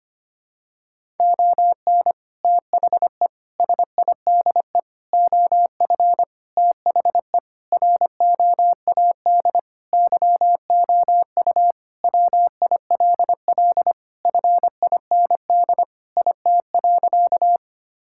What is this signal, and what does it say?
Morse code, heard as a keyed tone, about 25 wpm, 700 Hz. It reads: ON THE SIDE OF THE ROAD YOU WILL FIND IT.
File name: practice / 25wpm / Sentences with 3-Word Phrases - 25wpm